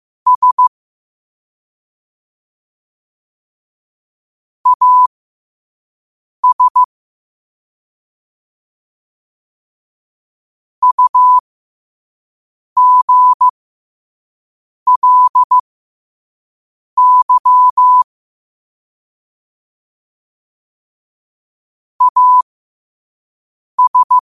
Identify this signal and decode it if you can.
Morse code, heard as a keyed tone, about 15 wpm, spaced as for 5 wpm, 1000 Hz. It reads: S AS UGLY AS
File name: qotdfiles / 5wpm